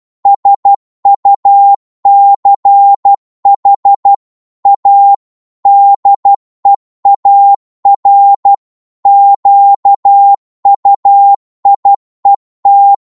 Morse code, heard as a keyed tone, about 12 wpm, 800 Hz. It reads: SUCH A DEAR QUIET